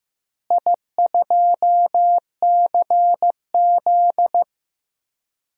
Morse code, heard as a keyed tone, about 15 wpm, 700 Hz. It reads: I2CZ
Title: DXCC-100 Call Signs - 4 Characters - Repeated 2x - 15wpm